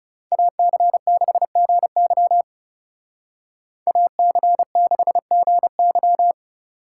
Morse code, heard as a keyed tone, about 36 wpm, 700 Hz. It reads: AC6GY AC6GY
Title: Speed Racing - USA 2x2 Call Signs 30wpm